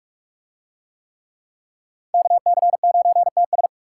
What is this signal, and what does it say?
Morse code, heard as a keyed tone, about 38 wpm, 700 Hz. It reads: KC9TS